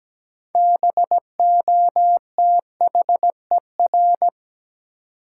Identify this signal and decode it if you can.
Morse code, heard as a keyed tone, about 17 wpm, 700 Hz. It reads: BOTHER
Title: Six Letter Words - 17wpm